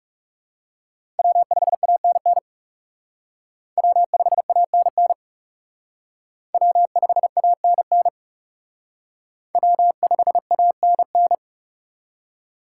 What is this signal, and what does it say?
Morse code, heard as a keyed tone, about 38 wpm, 700 Hz. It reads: W5ANN W5ANN W5ANN W5ANN